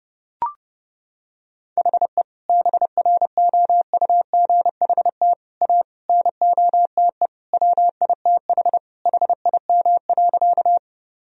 Morse code, heard as a keyed tone, about 30 wpm, 700 Hz. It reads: HE BROUGHT A NOTE WITH HIM.
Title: Sentences from Top 1000 Words - Review Part 2 - 30wpm